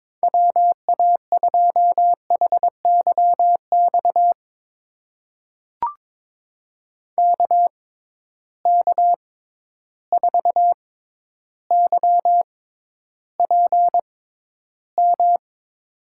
Morse code, heard as a keyed tone, about 22 wpm, spaced as for 11 wpm, 700 Hz. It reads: WA2HYX KK4YPM
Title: USA Call Signs - 6 Characters - 6x Character Spacing - 22wpm